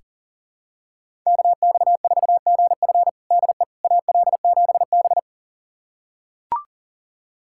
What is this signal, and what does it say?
Morse code, heard as a keyed tone, about 40 wpm, 700 Hz. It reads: KX4GF DE AL7B
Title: USA Callsign DE Callsign 40wpm